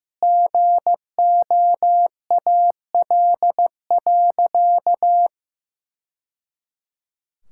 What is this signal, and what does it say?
Morse code, heard as a keyed tone, about 15 wpm, 700 Hz. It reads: GOAL.
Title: Sentences from Top 1000 Words - Review Part 5 - 15wpm